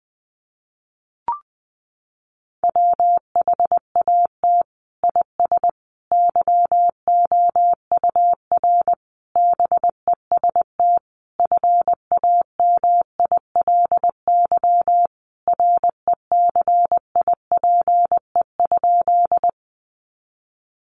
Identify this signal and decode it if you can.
Morse code, heard as a keyed tone, about 20 wpm, 700 Hz. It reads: WHAT IS YOUR BEST FAMILY RECIPE?
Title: Sentences from Top 1200 Words 20wpm